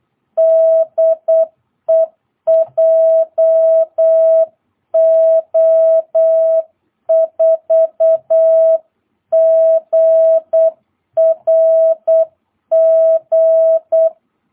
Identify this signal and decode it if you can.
Morse code, heard as a keyed tone, about 8 wpm, 650 Hz. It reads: DE JO4GRG
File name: jh8cbh / 2022cw